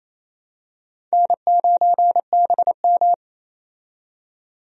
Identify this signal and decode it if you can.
Morse code, heard as a keyed tone, about 28 wpm, 700 Hz. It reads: N9BM
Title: USA Call Signs - 4 Characters - 28wpm